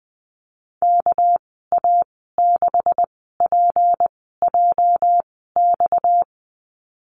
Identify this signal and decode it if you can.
Morse code, heard as a keyed tone, about 20 wpm, spaced as for 15 wpm, 700 Hz. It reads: KA6PJX
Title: 238 Lesson 38g - Letter q - Call Signs 20wpm